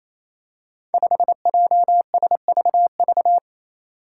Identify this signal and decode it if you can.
Morse code, heard as a keyed tone, about 27 wpm, 700 Hz. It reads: 5JSVV
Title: Five Challenging Letter-Number Combinations 28wpm